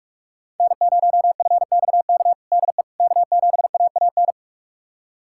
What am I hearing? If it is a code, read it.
Morse code, heard as a keyed tone, about 41 wpm, 700 Hz. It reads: N0FXK DE K7AAN
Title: USA Callsign DE Callsign - 5 Characters - 45wpm